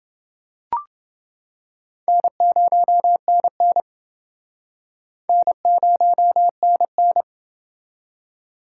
Morse code, heard as a keyed tone, about 29 wpm, 700 Hz. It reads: N0NN N0NN